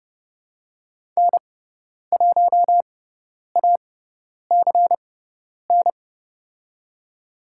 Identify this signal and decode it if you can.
Morse code, heard as a keyed tone, about 30 wpm, spaced as for 10 wpm, 690 Hz. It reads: N1ACN
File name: Lesson 18g - Letter m - Call Signs 30wpm